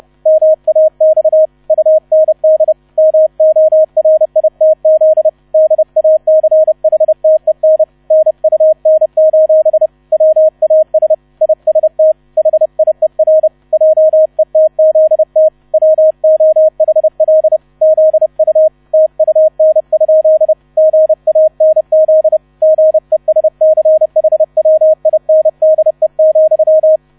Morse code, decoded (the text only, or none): MAX UND MORITZ DACHTEN NUN: WAS IST HIER JETZT WOHL ZU TUN? GANZ GESCHWINDE,